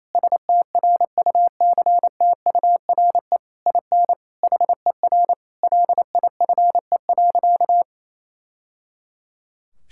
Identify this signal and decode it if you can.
Morse code, heard as a keyed tone, about 28 wpm, 700 Hz. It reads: STRUCTURE IN HER LIFE.
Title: Sentences from Top 1000 Words - Review Part 6 - 28wpm